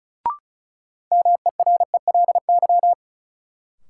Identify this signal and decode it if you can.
Morse code, heard as a keyed tone, about 36 wpm, 700 Hz. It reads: MERELY
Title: Sentences from Top 1400 Words - New Word Drill - 35wpm